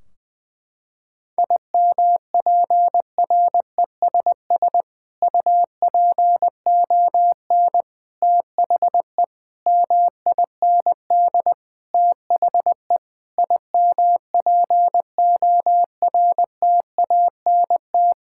Morse code, heard as a keyed tone, about 20 wpm, 700 Hz. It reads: IMPRESS UPON THE MIND THE IMPORTANT
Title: Sentences from Top 1600 Words 20wpm